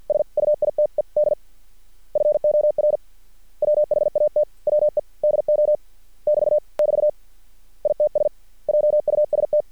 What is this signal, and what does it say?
Morse code, heard as a keyed tone, about 50 wpm, 600 Hz. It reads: SUITED FOR WHAT WE DO <BT> <BT> ITS JUST
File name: archived / CRSnet